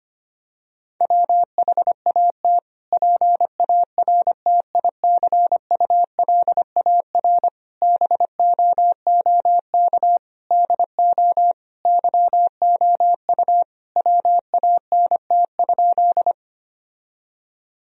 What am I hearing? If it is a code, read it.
Morse code, heard as a keyed tone, about 25 wpm, 700 Hz. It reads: WHAT PARTICULAR BOOK DO YOU WANT?
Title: Sentences from Top 1000 Words - Review Part 4 - 25wpm